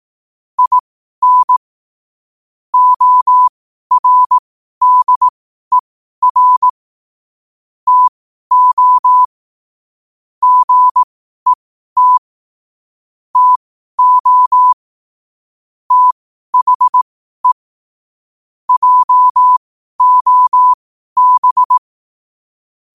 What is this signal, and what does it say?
Morse code, heard as a keyed tone, about 18 wpm, spaced as for 12 wpm, 1000 Hz. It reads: IN ORDER TO GET TO THE JOB
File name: qotdfiles / 12wpm